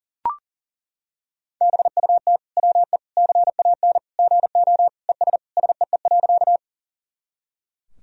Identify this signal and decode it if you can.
Morse code, heard as a keyed tone, about 40 wpm, 700 Hz. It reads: BUT WE CAN GO ES SEE.
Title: Sentences from Top 100 Words and Abbreviations 40wpm